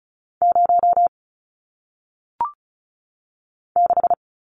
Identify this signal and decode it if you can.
Morse code, heard as a keyed tone, about 36 wpm, 700 Hz. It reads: 0 6
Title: Numbers - 1-Digit 35wpm